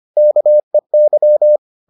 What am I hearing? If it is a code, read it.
Morse code, heard as a keyed tone, about 25 wpm, 600 Hz. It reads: KEY